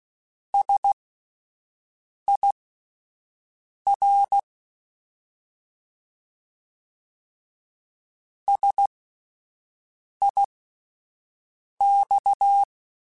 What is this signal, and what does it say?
Morse code, heard as a keyed tone, about 16 wpm, spaced as for 5 wpm, 800 Hz. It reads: SIR SIX